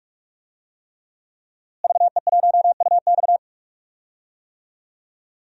Morse code, heard as a keyed tone, about 46 wpm, 700 Hz. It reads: VE1UX
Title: DXCC-100 Call Signs - 5 Characters - Repeated 3x - 45wpm